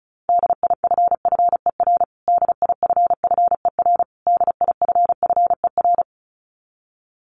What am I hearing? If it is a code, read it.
Morse code, heard as a keyed tone, about 34 wpm, 700 Hz. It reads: DIFFER DIFFER DIFFER
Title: Top 300 Words Repeated 35wpm